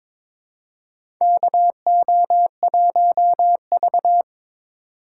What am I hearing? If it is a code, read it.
Morse code, heard as a keyed tone, about 22 wpm, 700 Hz. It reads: KO1V
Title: USA Call Signs - 4 Characters - 4x Character Spacing - 22wpm